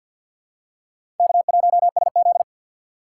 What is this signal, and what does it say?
Morse code, heard as a keyed tone, about 53 wpm, 700 Hz. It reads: K1SZ
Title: USA Call Signs - 4 Characters - 3x Character Spacing - 50wpm